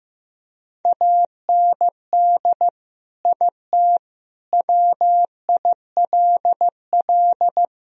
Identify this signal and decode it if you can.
Morse code, heard as a keyed tone, about 16 wpm, 700 Hz. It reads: AND IT WILL
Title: Sentences from Top 1000 Words - Review Part 1 - 15wpm